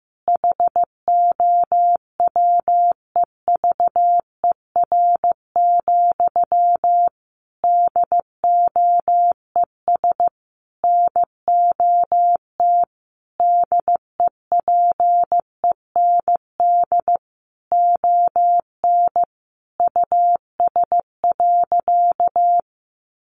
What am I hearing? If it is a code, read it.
Morse code, heard as a keyed tone, about 15 wpm, 700 Hz. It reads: HOWEVER, DOES NOT DEPEND ON US.
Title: Sentences from Top 1300 Words 15wpm